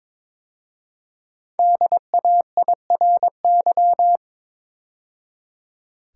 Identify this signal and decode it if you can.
Morse code, heard as a keyed tone, about 23 wpm, 700 Hz. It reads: DAIRY
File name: 5 Letter Words - First 3 Letters Given - 22wpm